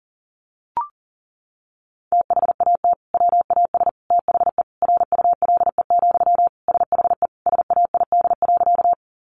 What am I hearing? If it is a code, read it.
Morse code, heard as a keyed tone, about 40 wpm, 700 Hz. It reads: THAT WAS THE RULE, SHE SAID.
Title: Sentences from Top 400 Words 40wpm